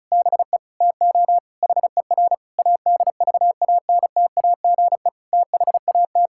Morse code, heard as a keyed tone, about 36 wpm, 700 Hz. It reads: BE TO HER ADVANTAGE THAT
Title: Sentences from Top 1000 Words - Review Part 6 - 35wpm